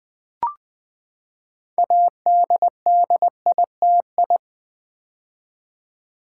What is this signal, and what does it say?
Morse code, heard as a keyed tone, about 20 wpm, 700 Hz. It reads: ADDITI
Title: Build A Word - 6 Letters At A Time - 12-Letter Words - 20wpm